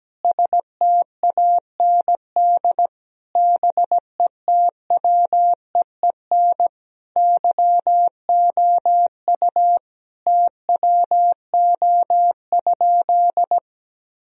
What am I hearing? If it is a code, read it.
Morse code, heard as a keyed tone, about 17 wpm, 700 Hz. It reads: STAND BETWEEN YOU TWO?
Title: Sentences from Top 1000 Words - Review Part 1 - 17wpm